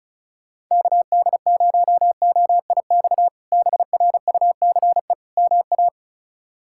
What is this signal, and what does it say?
Morse code, heard as a keyed tone, about 34 wpm, 700 Hz. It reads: KD0OIX BRUCE MA
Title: K1USN SST QSO Practice - 35wpm